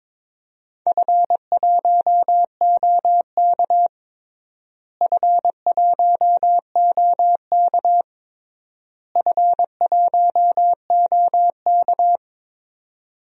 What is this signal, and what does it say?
Morse code, heard as a keyed tone, about 22 wpm, 700 Hz. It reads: F1OK F1OK F1OK
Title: DXCC-100 Call Signs - 4 Characters - Repeated 3x - 22wpm